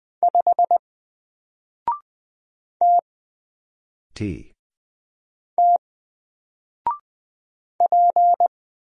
Morse code, heard as a keyed tone, about 20 wpm, 700 Hz. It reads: 5 T T P
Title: Single Letter-Number 20wpm